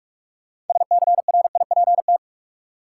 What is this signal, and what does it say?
Morse code, heard as a keyed tone, about 46 wpm, 700 Hz. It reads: SCRIPT
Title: Six Letter Words - 45wpm